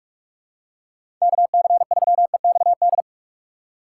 Morse code, heard as a keyed tone, about 42 wpm, 700 Hz. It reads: KC3EXD
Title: USA Call Signs - 6 Characters - 6x Character Spacing - 45wpm